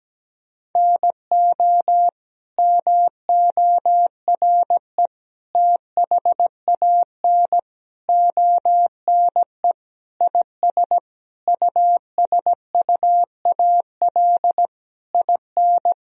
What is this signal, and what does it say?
Morse code, heard as a keyed tone, about 17 wpm, 700 Hz. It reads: NO MORE THAN ONE IS USUAL IN
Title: Sentences from Top 1000 Words - Review Part 2 - 17wpm